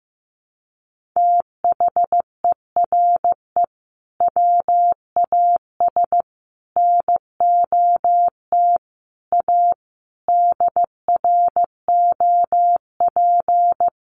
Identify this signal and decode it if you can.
Morse code, heard as a keyed tone, about 15 wpm, 700 Hz. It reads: THERE WAS NOT A DROP
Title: Sentences from Top 1400 Words 15wpm